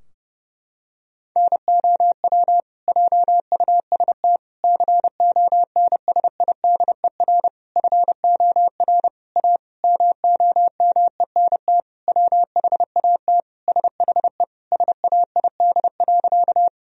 Morse code, heard as a keyed tone, about 30 wpm, 700 Hz. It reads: NOW JUST CONSIDER FOR A MOMENT WHAT SHE SAID.
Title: Sentences from Top 1000 Words - Review Part 4 - 30wpm